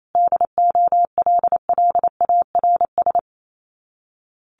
Morse code, heard as a keyed tone, about 28 wpm, 700 Hz. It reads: DOLLARS
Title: Sets of 5 Words Encore 28wpm